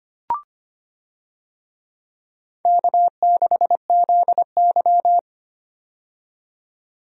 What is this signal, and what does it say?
Morse code, heard as a keyed tone, about 25 wpm, 700 Hz. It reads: K6ZY
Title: USA Call Signs - 4 Characters - 25wpm